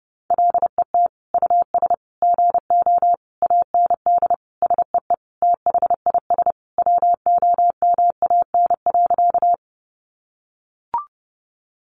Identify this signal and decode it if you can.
Morse code, heard as a keyed tone, about 30 wpm, 700 Hz. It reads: LET US GO AND SEE THIS WOMAN.
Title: Sentences from Top 1100 Words 30wpm